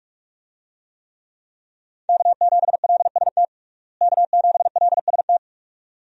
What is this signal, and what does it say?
Morse code, heard as a keyed tone, about 44 wpm, 700 Hz. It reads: K7LST K7LST
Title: USA Call Signs - 5 Characters - Repeated 2x - 45wpm